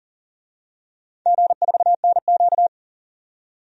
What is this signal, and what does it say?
Morse code, heard as a keyed tone, about 42 wpm, 700 Hz. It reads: G4NQ
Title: DXCC-100 Call Signs - 4 Characters - 40wpm